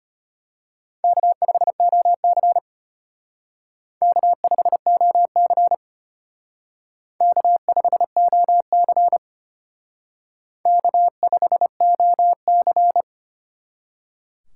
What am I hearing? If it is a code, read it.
Morse code, heard as a keyed tone, about 32 wpm, 700 Hz. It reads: K5OC K5OC K5OC K5OC